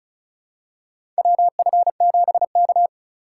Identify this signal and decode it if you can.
Morse code, heard as a keyed tone, about 36 wpm, 700 Hz. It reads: WF7K
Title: USA Call Signs - 4x Character Spacing - 35wpm